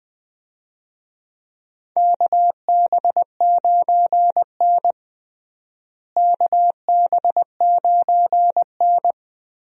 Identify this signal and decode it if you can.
Morse code, heard as a keyed tone, about 20 wpm, 700 Hz. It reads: KB9N KB9N
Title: USA Call Signs - 4 Characters - Repeated 2x - 20wpm